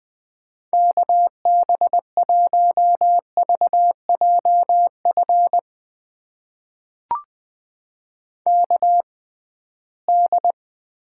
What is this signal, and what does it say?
Morse code, heard as a keyed tone, about 20 wpm, spaced as for 15 wpm, 700 Hz. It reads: KB1VJF KD
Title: USA Call Signs - 6 Characters - 6x Character Spacing - 20wpm